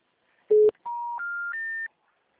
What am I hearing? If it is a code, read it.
Special information tone.